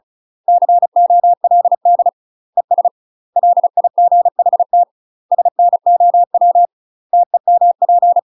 Morse code, heard as a keyed tone, about 36 wpm, 700 Hz. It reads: COLD ES LIGHT SNOW TEMP
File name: UK MASTER QSOs TRAINING / 35WPM / UK MASTER QSOs 0009